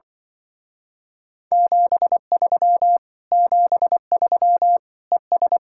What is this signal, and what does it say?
Morse code, heard as a keyed tone, about 25 wpm, 700 Hz. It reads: 73 73 ES